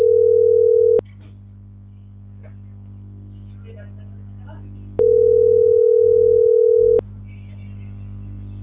Ringback tone (North America), 440 + 480 Hz.